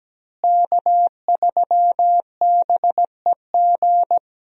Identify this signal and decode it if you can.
Morse code, heard as a keyed tone, about 17 wpm, 700 Hz. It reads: K3BEG
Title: USA Call Signs - 5 Characters - Repeated 2x - 17wpm